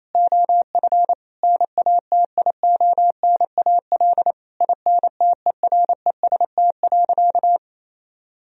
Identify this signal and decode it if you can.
Morse code, heard as a keyed tone, about 28 wpm, 700 Hz. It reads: OF NATIONAL INTEREST.
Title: Sentences from Top 1000 Words - Review Part 4 - 28wpm